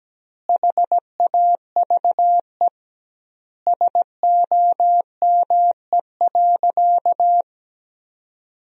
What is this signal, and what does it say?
Morse code, heard as a keyed tone, about 17 wpm, 700 Hz. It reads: HAVE SOME.
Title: 2x Word-Spacing - Sentences from Top 100 Words - 17wpm